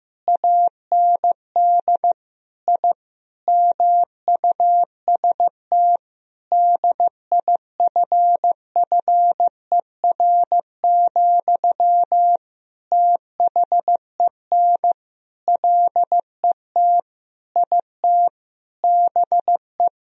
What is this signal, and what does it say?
Morse code, heard as a keyed tone, about 15 wpm, 700 Hz. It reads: AND I MUST DIFFER, THEN LET IT BE
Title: Sentences from Top 1000 Words - Review Part 1 - 15wpm